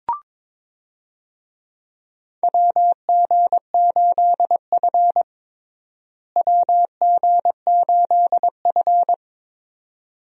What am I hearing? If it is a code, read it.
Morse code, heard as a keyed tone, about 22 wpm, 700 Hz. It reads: WG8F WG8F
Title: USA Call Signs - 4 Characters - Repeated 2x - 22wpm